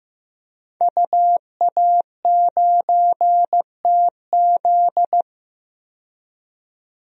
Morse code, heard as a keyed tone, about 15 wpm, 700 Hz. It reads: UA9TZ
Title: DXCC-100 Call Signs - 5 Characters - 15wpm